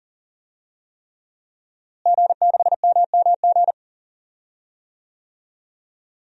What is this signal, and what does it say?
Morse code, heard as a keyed tone, about 40 wpm, 700 Hz. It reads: G6MMG
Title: DXCC-100 Call Signs - 5 Characters - Repeated 3x - 40wpm